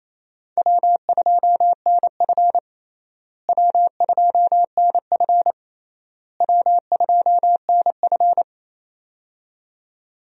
Morse code, heard as a keyed tone, about 28 wpm, 700 Hz. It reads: W2NF W2NF W2NF
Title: USA Call Signs - 4 Characters - Repeated 3x - 28wpm